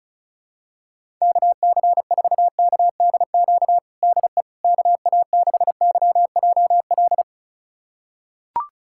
Morse code, heard as a keyed tone, about 36 wpm, 700 Hz. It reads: KC4KDQ DE KA6YJL
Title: USA Callsign DE Callsign - 6 Characters - 35wpm